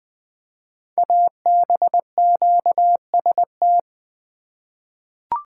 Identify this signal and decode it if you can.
Morse code, heard as a keyed tone, about 20 wpm, 700 Hz. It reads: ABQST
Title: Code Groups - 5 Letters - 20wpm